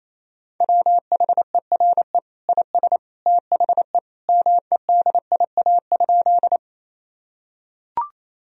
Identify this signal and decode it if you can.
Morse code, heard as a keyed tone, about 28 wpm, 700 Hz. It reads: WHERE IS THE MEDIA?
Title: Sentences from Top 1000 Words - Review Part 4 - 28wpm